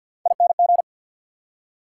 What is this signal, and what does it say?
Morse code, heard as a keyed tone, about 48 wpm, 690 Hz. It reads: ING
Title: Prefixes and Suffixes 50wpm